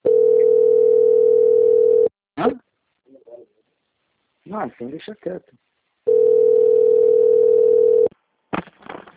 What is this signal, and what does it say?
Ringback tone (North America), 440 + 480 Hz, with speech.
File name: out- - - - - .51582